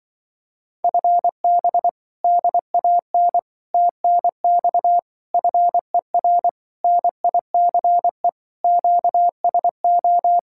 Morse code, heard as a keyed tone, about 24 wpm, 700 Hz. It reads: FB DAN TNX FER NICE QSO
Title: Speed-Racing Common QSO Phrases Encore - 20wpm